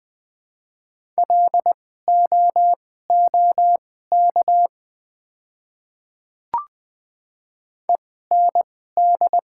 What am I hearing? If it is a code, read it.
Morse code, heard as a keyed tone, about 20 wpm, spaced as for 15 wpm, 690 Hz. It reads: LOOK END
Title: 2x Character Spacing - Top 200 Words - 20wpm